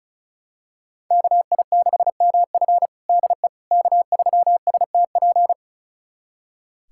Morse code, heard as a keyed tone, about 36 wpm, 700 Hz. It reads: KI6MF DE K3STP
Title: USA Callsign DE Callsign - 5 Characters - 35wpm